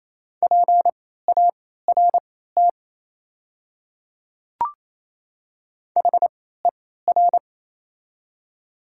Morse code, heard as a keyed tone, about 28 wpm, spaced as for 16 wpm, 700 Hz. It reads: PART HER
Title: 3x Character Spacing - Top 200 Words - 28wpm